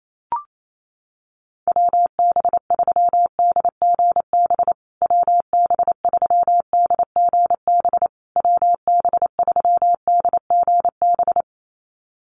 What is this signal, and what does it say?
Morse code, heard as a keyed tone, about 28 wpm, 700 Hz. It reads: WB3DGB WB3DGB WB3DGB
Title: USA Call Signs Repeated 3x 28wpm